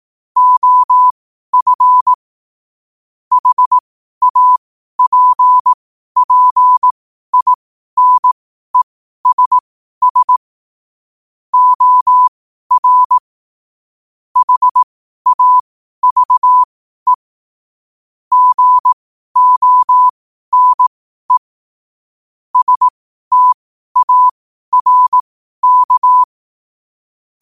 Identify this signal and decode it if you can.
Morse code, heard as a keyed tone, about 18 wpm, spaced as for 12 wpm, 1000 Hz. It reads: OF HAPPINESS OR HAVE GONE STARK
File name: qotdfiles / 12wpm